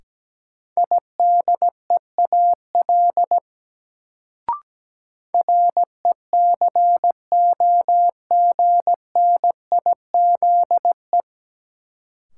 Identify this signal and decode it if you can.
Morse code, heard as a keyed tone, about 17 wpm, 700 Hz. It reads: IDEAL RECOGNIZE
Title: Sentences from Top 1400 Words - New Word Drill Alt - 17wpm